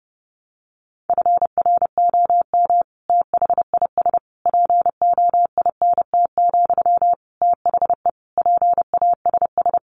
Morse code, heard as a keyed tone, about 30 wpm, 700 Hz. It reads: FROM THIS POINT, THE PASS
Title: Sentences from Top 1200 Words 30wpm